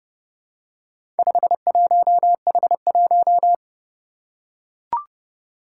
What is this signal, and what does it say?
Morse code, heard as a keyed tone, about 30 wpm, 700 Hz. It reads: 51H1
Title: Four Challenging Letter-Number Combinations 30wpm